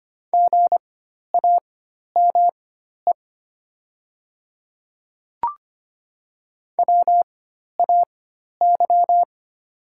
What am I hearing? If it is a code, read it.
Morse code, heard as a keyed tone, about 25 wpm, spaced as for 12 wpm, 700 Hz. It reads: GAME WAY
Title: 4x Character Spacing - Top 500 Words - 25wpm